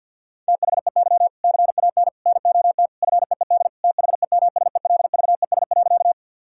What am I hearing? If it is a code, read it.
Morse code, heard as a keyed tone, about 51 wpm, 700 Hz. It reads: THEY CAN NOT FEED THEMSELVES.
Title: Sentences from Top 1600 Words 50wpm